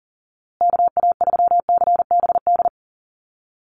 Morse code, heard as a keyed tone, about 40 wpm, 700 Hz. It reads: KA3CBD
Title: USA Call Signs Repeated 3x 40wpm